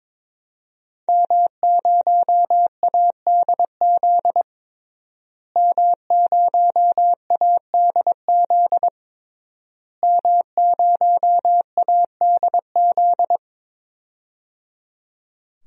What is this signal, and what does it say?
Morse code, heard as a keyed tone, about 22 wpm, 700 Hz. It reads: M0ADZ M0ADZ M0ADZ
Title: DXCC-100 Call Signs - 5 Characters - Repeated 3x - 22wpm